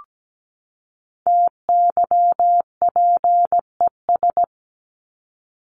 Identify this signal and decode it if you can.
Morse code, heard as a keyed tone, about 17 wpm, 700 Hz. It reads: TYPES
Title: ES:S Suffix - Top 100 Words with Base Words 17wpm